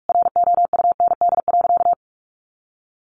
Morse code, heard as a keyed tone, about 44 wpm, 700 Hz. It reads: ROUND.
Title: Sentences from Top 200 Words 45wpm